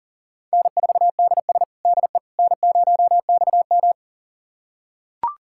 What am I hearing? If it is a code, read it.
Morse code, heard as a keyed tone, about 40 wpm, 700 Hz. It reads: N4DS DE N0XM
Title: USA Callsign DE Callsign - 4 Characters - 40wpm